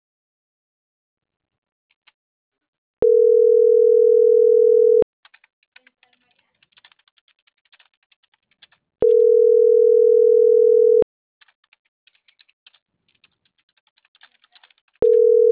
Ringback tone (North America), 440 + 480 Hz.